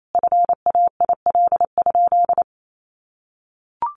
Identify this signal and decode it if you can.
Morse code, heard as a keyed tone, about 28 wpm, 700 Hz. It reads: FAIL?
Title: Sentences from Top 1100 Words 28wpm